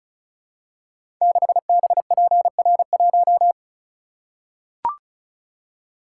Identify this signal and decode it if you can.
Morse code, heard as a keyed tone, about 36 wpm, 700 Hz. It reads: 6BPR1
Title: Code Groups - 5 Letter:Numbers - 35wpm